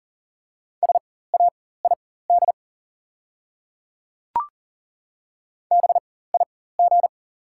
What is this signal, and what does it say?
Morse code, heard as a keyed tone, about 40 wpm, spaced as for 19 wpm, 700 Hz. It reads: SAID BIG
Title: 4x Character Spacing - Top 200 Words - 40wpm